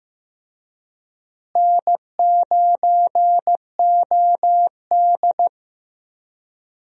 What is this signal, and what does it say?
Morse code, heard as a keyed tone, about 15 wpm, 700 Hz. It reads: N9OD
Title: USA Call Signs - 4 Characters - Repeated 2x - 15wpm